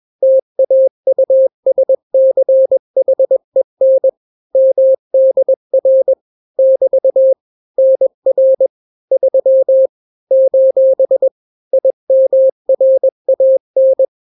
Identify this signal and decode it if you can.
Morse code, heard as a keyed tone, about 21 wpm, 530 Hz. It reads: TAUSCHEN MDR <BT> NR 3 : IMRAN
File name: daily / 20wpm